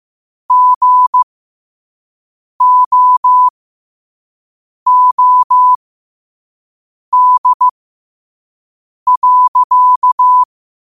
Morse code, heard as a keyed tone, about 15 wpm, spaced as for 5 wpm, 1000 Hz. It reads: GOOD.
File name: qotdfiles / 5wpm